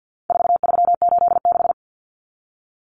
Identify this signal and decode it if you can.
Morse code, heard as a keyed tone, about 50 wpm, 700 Hz. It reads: 4386
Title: Numbers - 4-Digit 50wpm